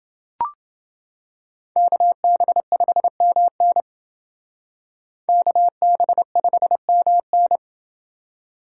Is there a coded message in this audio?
Yes: Morse code from 1.8 to 7.6 s, heard as a keyed tone, about 29 wpm, 700 Hz. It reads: KB5MN KB5MN